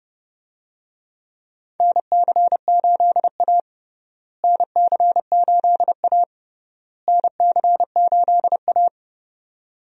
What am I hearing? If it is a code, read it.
Morse code, heard as a keyed tone, about 30 wpm, 700 Hz. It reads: NC8A NC8A NC8A
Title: USA Call Signs - 4 Characters - Repeated 3x - 30wpm